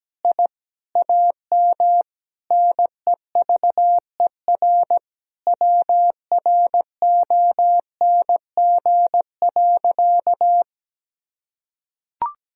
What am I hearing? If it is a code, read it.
Morse code, heard as a keyed tone, about 17 wpm, 700 Hz. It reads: I AM NEVER WRONG.
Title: Sentences from Top 1000 Words - Review Part 3 - 17wpm